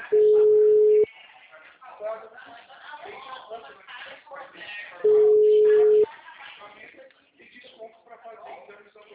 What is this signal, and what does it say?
Ringback tone (Europe, 425 Hz), with speech.